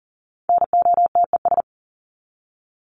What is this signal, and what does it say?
Morse code, heard as a keyed tone, about 40 wpm, 700 Hz. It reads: NOTES
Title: ES:S Suffix - Top 100 Words with Base Words 40wpm